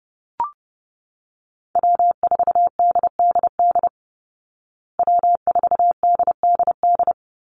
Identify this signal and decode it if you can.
Morse code, heard as a keyed tone, about 30 wpm, 700 Hz. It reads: W4DDD W4DDD
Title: Call Signs Repeated 2x - 30wpm